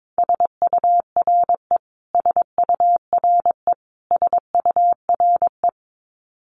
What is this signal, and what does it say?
Morse code, heard as a keyed tone, about 22 wpm, 700 Hz. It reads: SURE SURE SURE
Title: Top 500 Words Repeated 22wpm